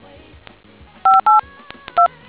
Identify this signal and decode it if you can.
DTMF, heard as telephone touch tones, tones of about 160 ms. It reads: 582